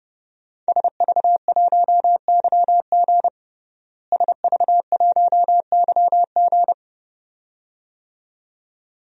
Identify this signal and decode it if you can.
Morse code, heard as a keyed tone, about 30 wpm, 700 Hz. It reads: SV1YG SV1YG
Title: DXCC-100 Call Signs - 5 Characters - Repeated 2x - 30wpm